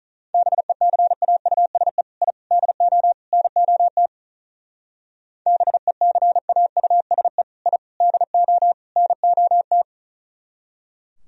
Morse code, heard as a keyed tone, about 44 wpm, 700 Hz. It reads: BECAUSE I DO NOT BECAUSE I DO NOT
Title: Speed-Racing - Sets of 4 Words 35wpm